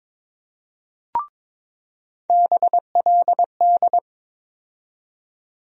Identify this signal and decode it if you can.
Morse code, heard as a keyed tone, about 21 wpm, 700 Hz. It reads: BLD
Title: USA Call Sign Postfixes - 3 Letters 22wpm